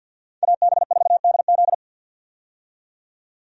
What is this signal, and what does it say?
Morse code, heard as a keyed tone, about 51 wpm, 700 Hz. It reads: AB4DZ
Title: USA Call Signs - 5 Characters - 50wpm